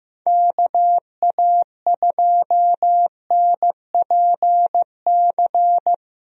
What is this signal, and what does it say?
Morse code, heard as a keyed tone, about 15 wpm, 700 Hz. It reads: KA2NPC
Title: USA Call Signs - 6 Characters - 15wpm